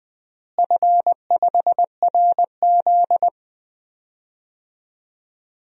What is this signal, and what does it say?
Morse code, heard as a keyed tone, about 20 wpm, 700 Hz. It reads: F5RZ
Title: DXCC-100 Call Signs - 4 Characters - Repeated 3x - 20wpm